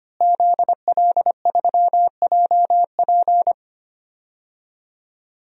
Morse code, heard as a keyed tone, about 25 wpm, 700 Hz. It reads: ZL3JP
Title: DXCC-100 Call Signs - 5 Characters - Repeated 2x - 25wpm